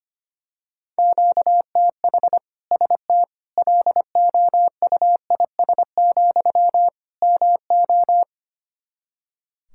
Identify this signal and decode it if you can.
Morse code, heard as a keyed tone, about 25 wpm, 700 Hz. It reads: QTH ST LOUIS, MO
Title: Practice QSO - Exchange 1 - Beginning - 25wpm